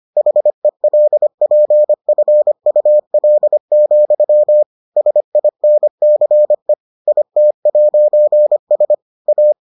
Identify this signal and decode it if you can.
Morse code, heard as a keyed tone, about 25 wpm, 600 Hz. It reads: HELPFUL, SINCE IT'S A